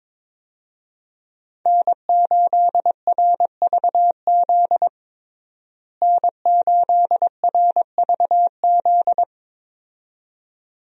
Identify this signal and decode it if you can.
Morse code, heard as a keyed tone, about 22 wpm, 700 Hz. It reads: N8RVZ N8RVZ
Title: USA Call Signs - 5 Characters - Repeated 2x - 22wpm